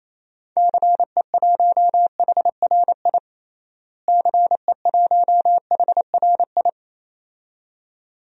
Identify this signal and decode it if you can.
Morse code, heard as a keyed tone, about 28 wpm, 700 Hz. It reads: CE1HRI CE1HRI
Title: DXCC-100 Call Signs - 6 Characters - Repeated 2x - 28wpm